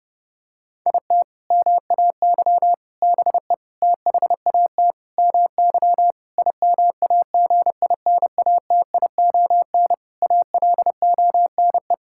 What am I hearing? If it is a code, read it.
Morse code, heard as a keyed tone, about 30 wpm, 700 Hz. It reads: IT MAY BE THAT MY IMAGINATION ALONE
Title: Sentences from Top 1600 Words 30wpm